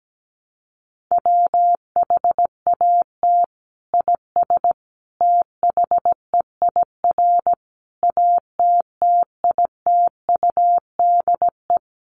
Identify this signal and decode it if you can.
Morse code, heard as a keyed tone, about 17 wpm, 700 Hz. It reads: WHAT IS THEIR ATTITUDE
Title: Sentences from Top 1300 Words 17wpm